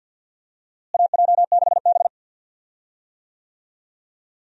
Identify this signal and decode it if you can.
Morse code, heard as a keyed tone, about 48 wpm, 700 Hz. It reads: AJ6B
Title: USA Call Signs - 4 Characters - Repeated 3x - 50wpm